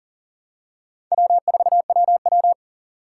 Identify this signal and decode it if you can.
Morse code, heard as a keyed tone, about 40 wpm, 700 Hz. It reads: W4WW
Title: USA Call Signs - 4 Characters - 3x Character Spacing - 40wpm